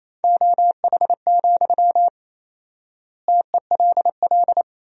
Morse code, heard as a keyed tone, about 28 wpm, 700 Hz. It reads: OH, TELL
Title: 4x Word-Spacing - Sentences from Top 500 Words - 28wpm